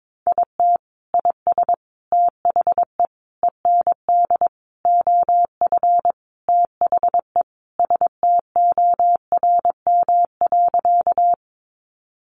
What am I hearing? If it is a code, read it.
Morse code, heard as a keyed tone, about 22 wpm, 700 Hz. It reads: IT IS THE END OF THE STORM.
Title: Sentences from Top 1500 Words 22wpm